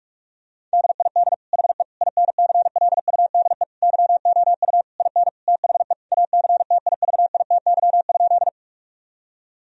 Morse code, heard as a keyed tone, about 44 wpm, 700 Hz. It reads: DID HE INCLUDE YOU IN THE ACTIVITY?
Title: Sentences from Top 1000 Words - Review Part 4 - 45wpm